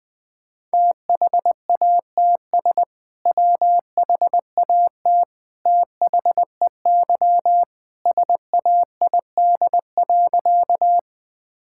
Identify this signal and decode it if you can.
Morse code, heard as a keyed tone, about 20 wpm, 700 Hz. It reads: THATS WHAT THEY SAID.
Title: Sentences with 2-Word Phrases - 20wpm